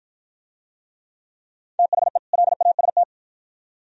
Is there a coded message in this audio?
Yes: Morse code from 1.8 to 3.0 s, heard as a keyed tone, about 53 wpm, 700 Hz. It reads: THE LAST